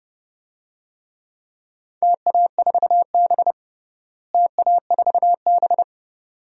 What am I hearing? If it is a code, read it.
Morse code, heard as a keyed tone, about 30 wpm, 700 Hz. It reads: TA4B TA4B
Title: DXCC-100 Call Signs - 4 Characters - Repeated 2x - 30wpm